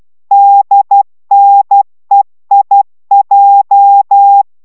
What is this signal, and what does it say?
Morse code, heard as a keyed tone, about 12 wpm, 800 Hz. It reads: DNEIJ